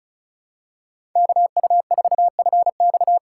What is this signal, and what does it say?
Morse code, heard as a keyed tone, about 36 wpm, 700 Hz. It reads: KU4FX
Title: USA Call Signs - 5 Characters - 2x Character Spacing - 35wpm